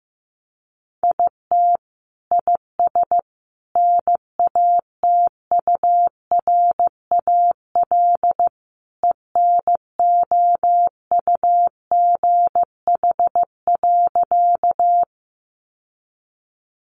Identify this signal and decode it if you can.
Morse code, heard as a keyed tone, about 14 wpm, 700 Hz. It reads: IT IS NATURAL ENOUGH.
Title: Sentences from Top 700 Words 15wpm